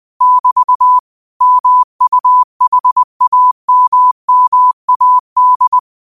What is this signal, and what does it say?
Morse code, heard as a keyed tone, about 20 wpm, 1000 Hz. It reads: <BT> MUHAMMAD